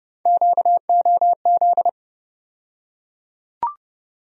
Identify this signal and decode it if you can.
Morse code, heard as a keyed tone, about 32 wpm, 700 Hz. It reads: QOZ
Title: USA Call Sign Postfixes - 3 Letters 30wpm